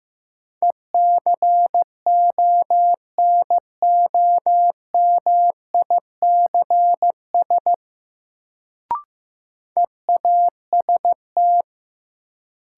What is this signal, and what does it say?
Morse code, heard as a keyed tone, about 15 wpm, 700 Hz. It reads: ECONOMICS EAST
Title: Sentences from Top 1000 Words - Review Part 3 - New Word Drill - 15wpm